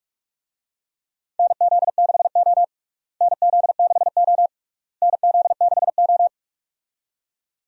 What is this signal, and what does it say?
Morse code, heard as a keyed tone, about 46 wpm, 700 Hz. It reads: NZ6O NZ6O NZ6O
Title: USA Call Signs - 4 Characters - Repeated 3x - 45wpm